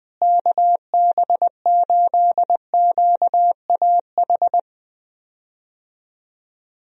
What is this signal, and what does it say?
Morse code, heard as a keyed tone, about 21 wpm, 700 Hz. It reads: KB8QAH
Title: USA Call Signs - 6 Characters - Repeated 2x - 20wpm